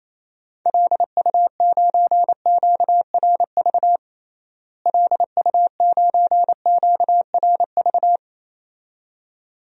Morse code, heard as a keyed tone, about 28 wpm, 700 Hz. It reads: LU9QRV LU9QRV
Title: DXCC-100 Call Signs - 6 Characters - Repeated 2x - 28wpm